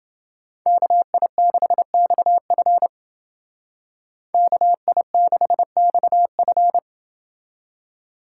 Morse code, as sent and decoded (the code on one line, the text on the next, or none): -.- .. -.... -..- ..-. / -.- .. -.... -..- ..-.
KI6XF KI6XF